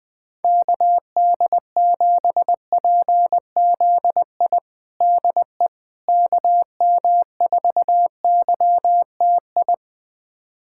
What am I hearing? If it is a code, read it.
Morse code, heard as a keyed tone, about 21 wpm, 700 Hz. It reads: KD7PZI DE KM4YTI
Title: USA Callsign DE Callsign - 6 Characters - 20wpm